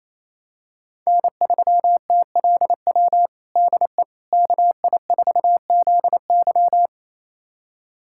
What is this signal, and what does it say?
Morse code, heard as a keyed tone, about 28 wpm, 700 Hz. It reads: N3TLW DE KI4ZY
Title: USA Callsign DE Callsign - 5 Characters - 28wpm